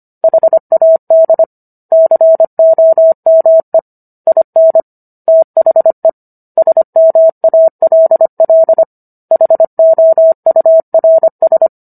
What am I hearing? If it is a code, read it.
Morse code, heard as a keyed tone, about 25 wpm, 650 Hz. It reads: HAD COME IN THE SMALL HOURS